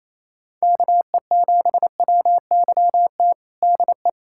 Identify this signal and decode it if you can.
Morse code, heard as a keyed tone, about 27 wpm, 700 Hz. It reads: KE7WYT DE
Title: USA Callsign DE Callsign 28wpm